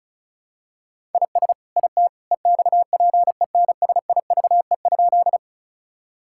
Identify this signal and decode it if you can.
Morse code, heard as a keyed tone, about 36 wpm, 700 Hz. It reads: IS IT EXPENSIVE?
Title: Sentences from Top 1000 Words - Review Part 6 - 35wpm